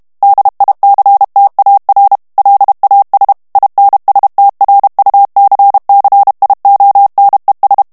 Morse code, heard as a keyed tone, about 32 wpm, 800 Hz. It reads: DICTAR LAS INSTRUCCIONES
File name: ESPAGNOL TEXTE EN CLAIR 2 VITESSE 30 MOTS MINUTES